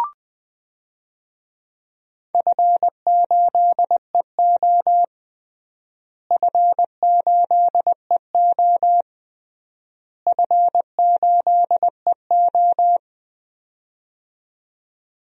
Morse code, heard as a keyed tone, about 20 wpm, 700 Hz. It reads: F8EO F8EO F8EO
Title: DXCC-100 Call Signs - 4 Characters - Repeated 3x - 20wpm